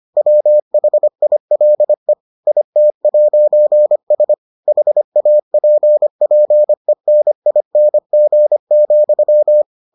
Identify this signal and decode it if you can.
Morse code, heard as a keyed tone, about 25 wpm, 600 Hz. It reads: WHILE IT'S HAPPENING,